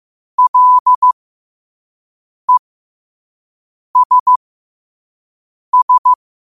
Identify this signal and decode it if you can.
Morse code, heard as a keyed tone, about 15 wpm, spaced as for 5 wpm, 1000 Hz. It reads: LESS
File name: qotdfiles / 5wpm